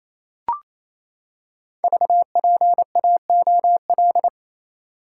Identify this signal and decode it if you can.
Morse code, heard as a keyed tone, about 27 wpm, 700 Hz. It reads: VPAOL
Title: Code Groups - 5 Letters - 28wpm